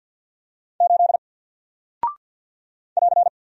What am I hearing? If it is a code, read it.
Morse code, heard as a keyed tone, about 48 wpm, 700 Hz. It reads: 8 <AR>
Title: Single Character (Letter:Number:Punctuation:Prosign) Alt 50wpm